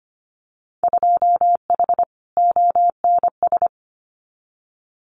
Morse code, heard as a keyed tone, about 25 wpm, 700 Hz. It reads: 2H ONS
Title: Winter Field Day Exchanges Encore 25wpm